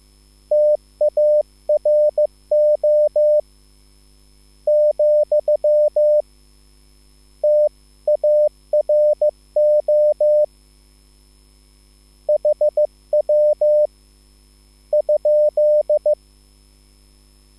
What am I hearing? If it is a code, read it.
Morse code, heard as a keyed tone, about 15 wpm, spaced as for 11 wpm, 600 Hz. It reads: TARO, TARO HW?
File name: QSL / CW lesson